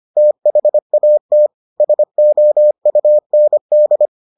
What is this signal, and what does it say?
Morse code, heard as a keyed tone, about 25 wpm, 600 Hz. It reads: THAT SOUND